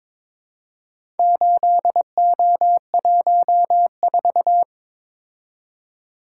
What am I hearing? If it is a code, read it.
Morse code, heard as a keyed tone, about 22 wpm, 700 Hz. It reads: 8O14
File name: practice / 22wpm / Four Letter-Number Combinations - 22wpm